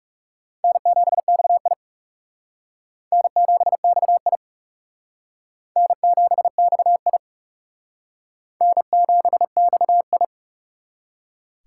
Morse code, heard as a keyed tone, about 38 wpm, 700 Hz. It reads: N7XI N7XI N7XI N7XI